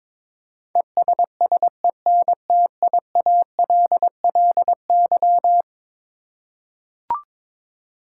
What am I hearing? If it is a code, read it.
Morse code, heard as a keyed tone, about 22 wpm, 700 Hz. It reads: ESSENTIALLY
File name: practice / 22wpm / Eleven Letter Words - 22wpm